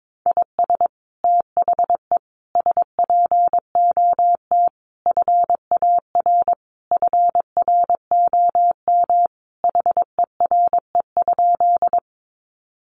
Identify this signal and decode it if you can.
Morse code, heard as a keyed tone, about 22 wpm, 700 Hz. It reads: IS THE SPOT FAR FROM HERE?
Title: Sentences from Top 1100 Words 22wpm